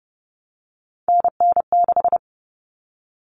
Morse code, heard as a keyed tone, about 30 wpm, 700 Hz. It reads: NN6
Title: Common Cut Numbers - 3-Digit 30wpm